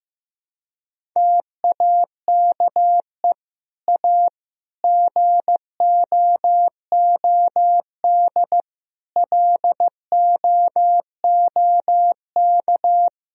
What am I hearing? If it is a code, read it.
Morse code, heard as a keyed tone, about 15 wpm, 700 Hz. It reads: TAKE A GOOD LOOK
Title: Sentences from Top 1000 Words - Review Part 3 - 15wpm